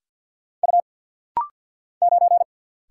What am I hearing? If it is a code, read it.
Morse code, heard as a keyed tone, about 49 wpm, 700 Hz. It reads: U 9
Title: Single Letter-Number - Rapid-Fire 50wpm